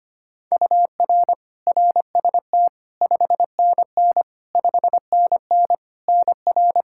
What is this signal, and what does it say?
Morse code, heard as a keyed tone, about 25 wpm, 700 Hz. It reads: UR RST 5NN 5NN NR
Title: Practice QSO - Exchange 1 - Beginning - 25wpm